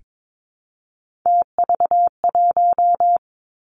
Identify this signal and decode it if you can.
Morse code, heard as a keyed tone, about 22 wpm, 700 Hz. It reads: TV1
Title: Three Challenging Letter-Number Combinations 22wpm